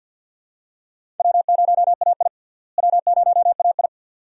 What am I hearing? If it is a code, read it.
Morse code, heard as a keyed tone, about 46 wpm, 700 Hz. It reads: W0AI W0AI
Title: USA Call Signs - 4 Characters - Repeated 2x - 50wpm